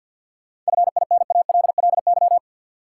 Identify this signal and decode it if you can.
Morse code, heard as a keyed tone, about 51 wpm, 700 Hz. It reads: FINALLY
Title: Sentences from Top 1000 Words - New Word Drill - 50wpm